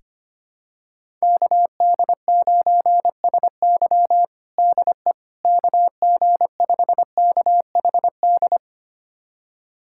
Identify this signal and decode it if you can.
Morse code, heard as a keyed tone, about 25 wpm, 700 Hz. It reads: KD9SY DE KG5KHD
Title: USA Callsign DE Callsign 25wpm